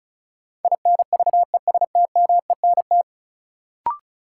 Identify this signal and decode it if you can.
Morse code, heard as a keyed tone, about 36 wpm, 700 Hz. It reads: INVESTMENT
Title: Sentences from Top 700 Words - New Word Drill - 35wpm